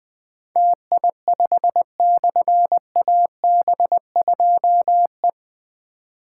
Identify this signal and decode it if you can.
Morse code, heard as a keyed tone, about 20 wpm, 700 Hz. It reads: TI5/AB2E
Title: Difficult Call Signs 20wpm